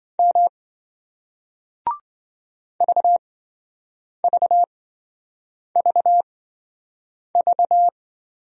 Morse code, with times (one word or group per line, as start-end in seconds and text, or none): 0.2-0.5 M
2.8-7.9 VVVV